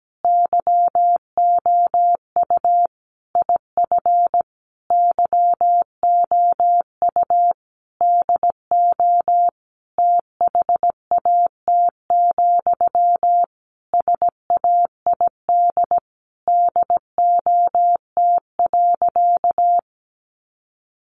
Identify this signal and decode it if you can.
Morse code, heard as a keyed tone, about 17 wpm, 700 Hz. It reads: YOU IF YOU DO THAT, SAID DOT.
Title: Sentences from Top 1500 Words 17wpm